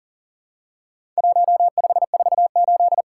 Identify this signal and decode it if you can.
Morse code, heard as a keyed tone, about 40 wpm, 700 Hz. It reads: 1548
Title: Worked All Europe DX Contest - QTC Exchange - 40wpm